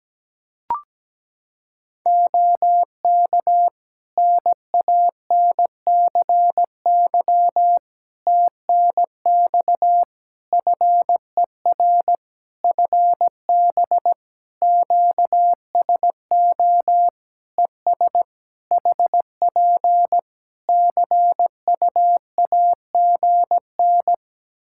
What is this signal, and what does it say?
Morse code, heard as a keyed tone, about 17 wpm, 700 Hz. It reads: OK NANCY TNX FER FB QSO ES HP CUAGN
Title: Practice QSO - Exchange 4 - Ending - 17wpm